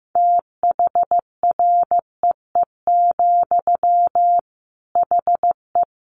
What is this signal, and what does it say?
Morse code, heard as a keyed tone, about 15 wpm, 700 Hz. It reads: THREE, HE
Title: Sentences from Top 200 Words 15wpm